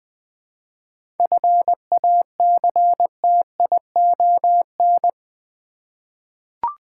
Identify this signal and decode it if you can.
Morse code, heard as a keyed tone, about 20 wpm, 700 Hz. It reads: FACTION
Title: 7 Letter Words - First 3 Letters Given - 20wpm